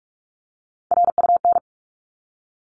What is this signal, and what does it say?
Morse code, heard as a keyed tone, about 44 wpm, 700 Hz. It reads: RUN
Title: Top 300 Words 45wpm